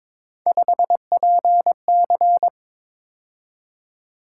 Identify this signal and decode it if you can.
Morse code, heard as a keyed tone, about 22 wpm, 700 Hz. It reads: 5PC
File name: practice / 22wpm / Three Letter-Number Combinations - 22wpm